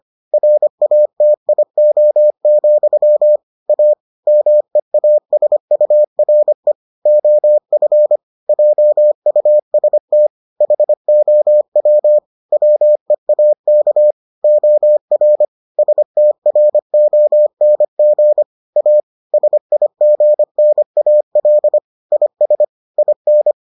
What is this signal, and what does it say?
Morse code, heard as a keyed tone, about 25 wpm, 600 Hz. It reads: RATIO, A MEASURE OF JUST HOW WEAK OR STRONG A SIGNAL IS IN